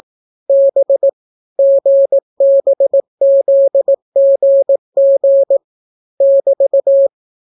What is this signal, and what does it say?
Morse code, heard as a keyed tone, about 18 wpm, 550 Hz. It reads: B GBZGG <BT>